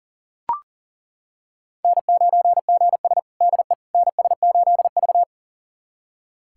Morse code, heard as a keyed tone, about 40 wpm, 700 Hz. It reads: N9GS DE NS8V
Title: USA Callsign DE Callsign - 4 Characters - 40wpm